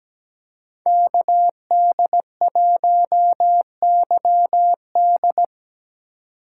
Morse code, heard as a keyed tone, about 17 wpm, 700 Hz. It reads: KD1YD
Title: USA Call Signs - 5 Characters - 2x Character Spacing - 17wpm